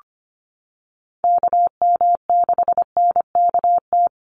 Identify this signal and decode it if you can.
Morse code, heard as a keyed tone, about 25 wpm, 700 Hz. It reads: KM6NKT
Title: USA Call Signs 25wpm